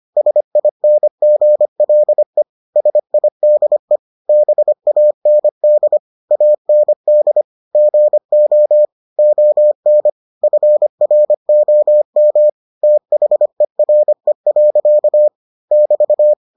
Morse code, heard as a keyed tone, about 25 wpm, 600 Hz. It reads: SINGLE SIDE BAND AND GO ON FROM THERE. <BT>